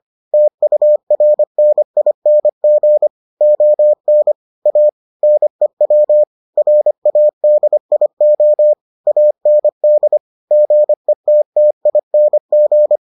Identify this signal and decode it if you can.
Morse code, heard as a keyed tone, about 25 wpm, 600 Hz. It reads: TURNING ON A NEW RADIO AND GETTING